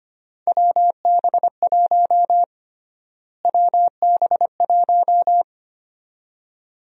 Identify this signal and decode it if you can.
Morse code, heard as a keyed tone, about 25 wpm, 700 Hz. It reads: WB1 WB1
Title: USA Call Sign Prefixes - Repeated 2x - 25wpm